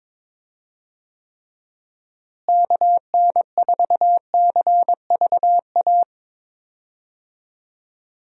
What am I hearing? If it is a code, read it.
Morse code, heard as a keyed tone, about 23 wpm, 700 Hz. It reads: KN4CVA
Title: USA Call Signs - 6 Characters - Repeated 2x - 22wpm